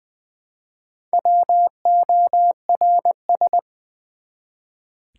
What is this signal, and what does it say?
Morse code, heard as a keyed tone, about 20 wpm, 690 Hz. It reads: WORS
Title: Build A Word - 1 Letter At A Time - 5-Letter Words - 20wpm